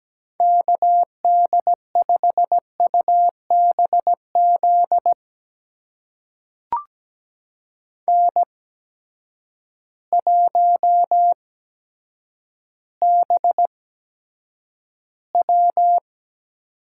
Morse code, heard as a keyed tone, about 17 wpm, spaced as for 9 wpm, 700 Hz. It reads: KD5UBZ N1BW
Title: USA Call Signs - 8x Character Spacing - 17wpm